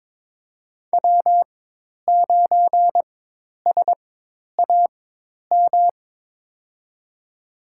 Morse code, heard as a keyed tone, about 22 wpm, spaced as for 10 wpm, 700 Hz. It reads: W9SAM
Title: USA Call Signs - 5 Characters - 4x Character Spacing - 22wpm